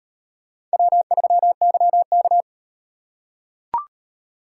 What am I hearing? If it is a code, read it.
Morse code, heard as a keyed tone, about 35 wpm, 700 Hz. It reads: W3YK